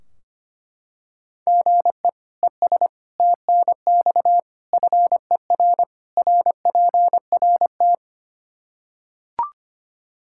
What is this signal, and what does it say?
Morse code, heard as a keyed tone, about 25 wpm, 700 Hz. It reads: GE ES TNX FER RPRT
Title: Practice QSO - Exchange 1 - Beginning - 25wpm